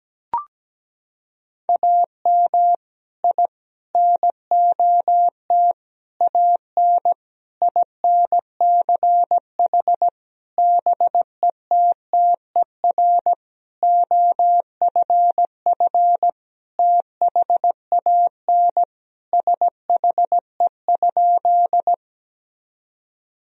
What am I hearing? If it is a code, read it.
Morse code, heard as a keyed tone, about 17 wpm, 700 Hz. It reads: AM I NOT AN INCH BETTER OFF THAN SHE?
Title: Sentences from Top 1000 Words - Review Part 3 - 17wpm